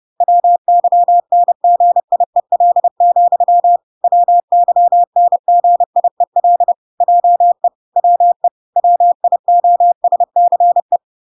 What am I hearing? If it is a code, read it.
Morse code, heard as a keyed tone, about 30 wpm, 700 Hz. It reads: WYNGIEL, WYNGIEL JE WE WIOSCE